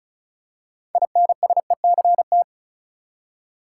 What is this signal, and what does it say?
Morse code, heard as a keyed tone, about 34 wpm, 700 Hz. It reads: INSECT
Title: Six Letter Words - 35wpm